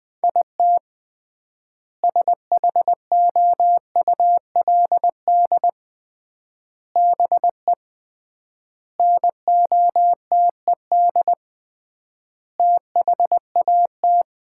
Morse code, heard as a keyed tone, about 20 wpm, 700 Hz. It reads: IT SHOULD BE NOTED THAT
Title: 3x Word-Spacing - Sets of 5 Words - 20wpm